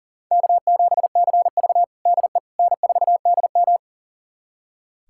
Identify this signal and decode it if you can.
Morse code, heard as a keyed tone, about 38 wpm, 700 Hz. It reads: K7CV DE N4DM
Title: USA Callsign DE Callsign - 4 Characters - 40wpm